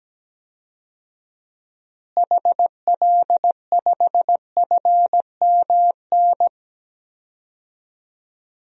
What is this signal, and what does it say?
Morse code, heard as a keyed tone, about 17 wpm, 700 Hz. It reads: HL5FMN
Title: DXCC-100 Call Signs - 6 Characters - Repeated 2x - 17wpm